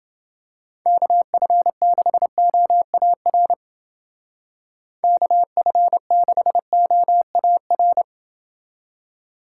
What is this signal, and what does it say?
Morse code, heard as a keyed tone, about 29 wpm, 700 Hz. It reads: KF6OAR KF6OAR